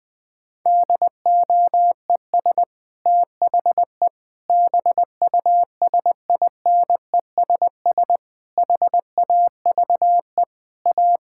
Morse code, heard as a keyed tone, about 20 wpm, 700 Hz. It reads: DOES THE BUSINESS HAVE A
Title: Sentences from Top 1600 Words 20wpm